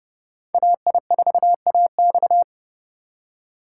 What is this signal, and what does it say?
Morse code, heard as a keyed tone, about 28 wpm, 700 Hz. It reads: AI4AX